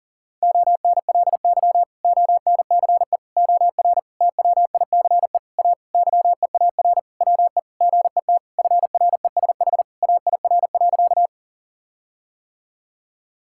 Morse code, heard as a keyed tone, about 40 wpm, 700 Hz. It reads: ONLY ONCE OR TWICE A YEAR WE GET FRESH AIR.
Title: Sentences from Top 1600 Words 40wpm